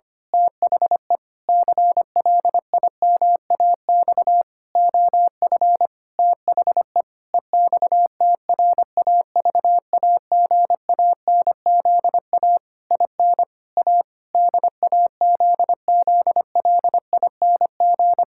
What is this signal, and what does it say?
Morse code, heard as a keyed tone, about 25 wpm, 700 Hz. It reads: THE CLIMAX OF THE EXTRAVAGANZA IN A DAZZLING
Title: Pangram Challenge v2 - 25wpm